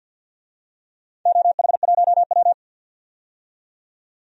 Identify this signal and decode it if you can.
Morse code, heard as a keyed tone, about 53 wpm, 700 Hz. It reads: OH1W
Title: DXCC-100 Call Signs - 4 Characters - 50wpm